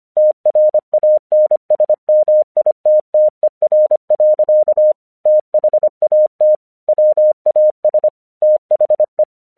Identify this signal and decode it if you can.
Morse code, heard as a keyed tone, about 26 wpm, 610 Hz. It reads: TRANSMITTER. THAT WAS THE
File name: 1 English 25wpm / AQC12